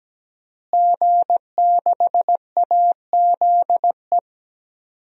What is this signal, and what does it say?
Morse code, heard as a keyed tone, about 17 wpm, 700 Hz. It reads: G6AZE
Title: DXCC-100 Call Signs - 5 Characters - Repeated 2x - 17wpm